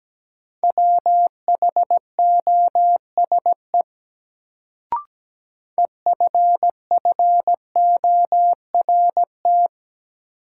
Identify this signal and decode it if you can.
Morse code, heard as a keyed tone, about 17 wpm, 700 Hz. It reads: WHOSE EFFORT
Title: Sentences from Top 1200 Words - New Word Drill Alt - 17wpm